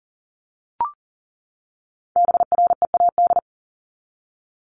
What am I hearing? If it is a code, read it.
Morse code, heard as a keyed tone, about 40 wpm, 700 Hz. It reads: BREAD
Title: Top 1501-1750 Words 40wpm